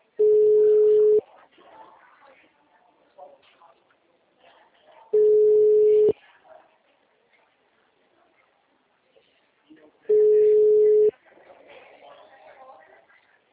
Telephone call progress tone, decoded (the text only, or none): ringback tone